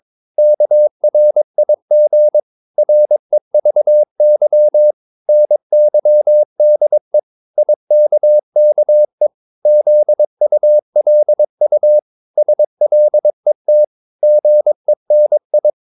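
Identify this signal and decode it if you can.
Morse code, heard as a keyed tone, about 22 wpm, 600 Hz. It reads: KRIG REVY NYDE IKKE ZULU SLET GENI
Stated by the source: Morse practice of short Danish words at 22 wpm, from CW med Gnister.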